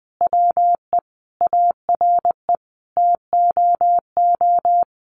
Morse code, heard as a keyed tone, about 20 wpm, 700 Hz. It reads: WE ARE TOO
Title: Sentences from Top 1400 Words 20wpm